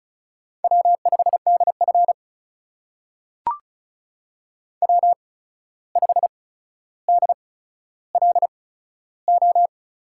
Morse code, heard as a keyed tone, about 36 wpm, spaced as for 14 wpm, 700 Hz. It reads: W5DF W5DLO
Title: USA Call Signs - 8x Character Spacing - 35wpm